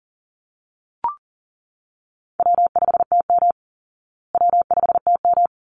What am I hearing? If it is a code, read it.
Morse code, heard as a keyed tone, about 40 wpm, 700 Hz. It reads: W5TM W5TM
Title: Call Signs Repeated 3x - 40wpm